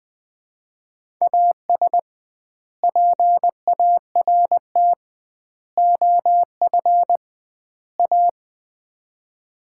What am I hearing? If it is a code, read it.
Morse code, heard as a keyed tone, about 21 wpm, 700 Hz. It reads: AS PART OF A
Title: 2x Word-Spacing - Sets of 4 Words - 20wpm